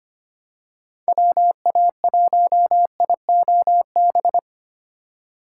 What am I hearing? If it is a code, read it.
Morse code, heard as a keyed tone, about 25 wpm, 700 Hz. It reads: WA1IOB
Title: USA Call Signs - 6 Characters - 3x Character Spacing - 25wpm